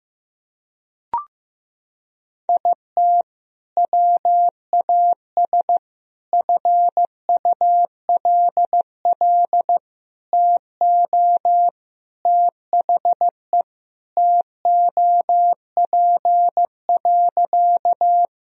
Morse code, heard as a keyed tone, about 15 wpm, 700 Hz. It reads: IT WAS FULL TO THE TOP.
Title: Sentences from Top 1000 Words - Review Part 3 - 15wpm